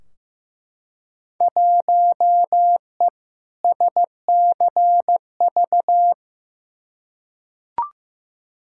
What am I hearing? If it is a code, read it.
Morse code, heard as a keyed tone, about 15 wpm, 700 Hz. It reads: 1E SCV
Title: Field Day Exchanges - 15wpm